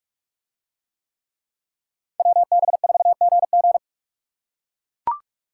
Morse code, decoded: WB4GG

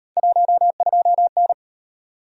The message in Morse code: 12N

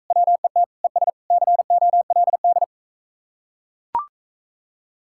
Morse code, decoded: WET ES COLD

